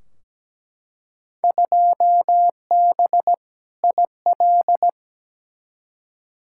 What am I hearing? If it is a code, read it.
Morse code, heard as a keyed tone, about 17 wpm, 700 Hz. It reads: 2B IL
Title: Field Day Exchanges Encore - 17wpm